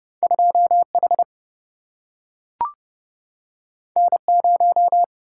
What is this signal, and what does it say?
Morse code, heard as a keyed tone, about 30 wpm, 700 Hz. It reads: 2H N0
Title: Single-Letter Single-Number 30wpm